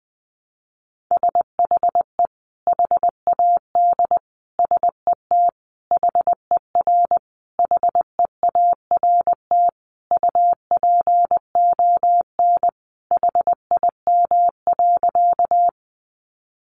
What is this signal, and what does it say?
Morse code, heard as a keyed tone, about 20 wpm, 700 Hz. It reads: SHE HAD SET HER HEART UPON HIM.
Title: Sentences from Top 1200 Words 20wpm